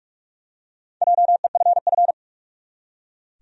Morse code, heard as a keyed tone, about 44 wpm, 700 Hz. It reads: JEFF
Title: USA Common Names - Max 4 Letters - 45wpm